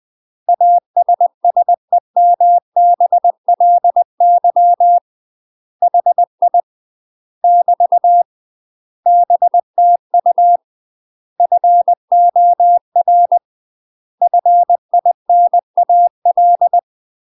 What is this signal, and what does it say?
Morse code, heard as a keyed tone, about 20 wpm, 700 Hz. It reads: ASSEMBLY HI <BT> BTU FOR FINAL